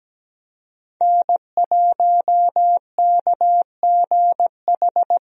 Morse code, heard as a keyed tone, about 17 wpm, 700 Hz. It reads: N1KGH
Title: USA Call Signs - 5 Characters - Repeated 2x - 17wpm